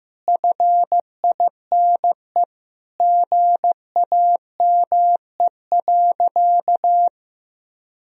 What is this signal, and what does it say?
Morse code, heard as a keyed tone, about 15 wpm, 700 Hz. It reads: FINE GAME.
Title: Sentences from Top 1000 Words - Review Part 3 - 15wpm